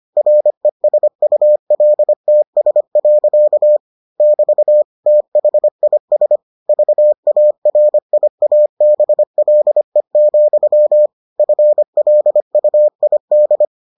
Morse code, heard as a keyed tone, about 25 wpm, 600 Hz. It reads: RESULTS. <BT> THIS VARIABLE, FLUID